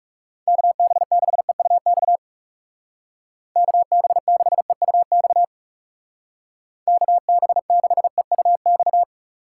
Morse code, heard as a keyed tone, about 37 wpm, 700 Hz. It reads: KB6EUX KB6EUX KB6EUX
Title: Speed Racing - USA 2x3 Call Signs 30wpm